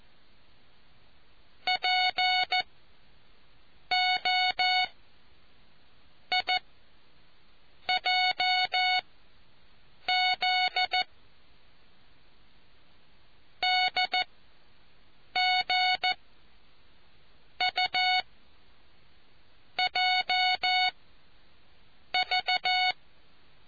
Morse code, heard as a keyed tone, about 14 wpm, spaced as for 5 wpm, 740 Hz. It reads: POIJZ DGUJV